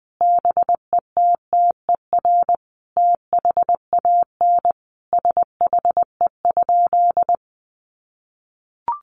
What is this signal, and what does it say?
Morse code, heard as a keyed tone, about 20 wpm, 700 Hz. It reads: BETTER THAN SHE?
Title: Sentences from Top 1200 Words 20wpm